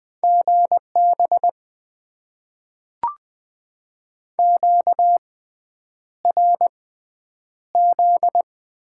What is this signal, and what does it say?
Morse code, heard as a keyed tone, about 20 wpm, spaced as for 9 wpm, 700 Hz. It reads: GB QRZ
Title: Top 90 QSO Elements - 6x Character Spacing - 20wpm